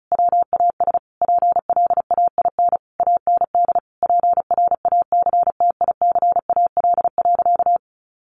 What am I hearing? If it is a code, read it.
Morse code, heard as a keyed tone, about 34 wpm, 700 Hz. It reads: WAS PLAIN AND PRACTICAL.
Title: Sentences from Top 1400 Words 35wpm